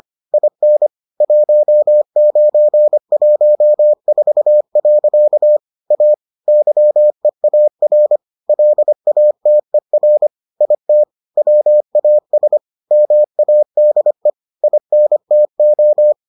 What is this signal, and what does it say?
Morse code, heard as a keyed tone, about 25 wpm, 600 Hz. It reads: IN 1914. A YEAR LATER IT WAS MADE INTO